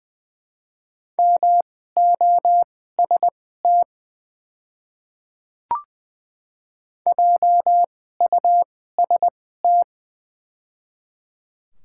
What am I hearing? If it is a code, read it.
Morse code, heard as a keyed tone, about 21 wpm, spaced as for 15 wpm, 700 Hz. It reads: MOST JUST
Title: 2x Character Spacing - Top 200 Words - 20wpm